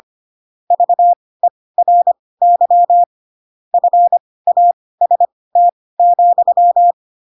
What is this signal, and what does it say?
Morse code, heard as a keyed tone, about 25 wpm, spaced as for 18 wpm, 700 Hz. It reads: VERY FAST,
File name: Pinocchio / Pinocchio ch5 12 25 ik1hge